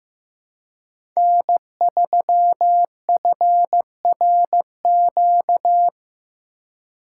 Morse code, heard as a keyed tone, about 15 wpm, 700 Hz. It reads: N3FRQ
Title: USA Call Signs - 5 Characters - 2x Character Spacing - 15wpm